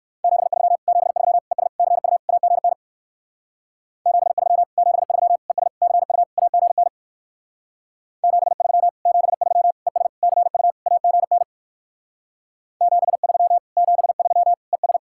Morse code, heard as a keyed tone, about 58 wpm, 700 Hz. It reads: 73 73 ES CU AGN 73 73 ES CU AGN 73 73 ES CU AGN 73 73 ES